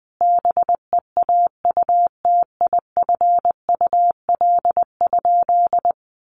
Morse code, heard as a keyed tone, about 20 wpm, 700 Hz. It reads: BEAUTIFUL?
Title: Sentences from Top 1200 Words 20wpm